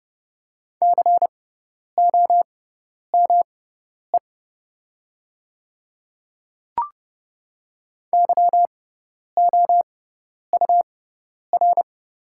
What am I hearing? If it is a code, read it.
Morse code, heard as a keyed tone, about 29 wpm, spaced as for 10 wpm, 700 Hz. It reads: COME YOUR
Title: 6x Character Spacing - Top 300 Words - 30wpm